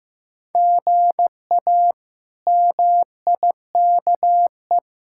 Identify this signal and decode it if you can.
Morse code, heard as a keyed tone, about 15 wpm, 700 Hz. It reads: GA MIKE
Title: K1USN SST QSO Practice - 15wpm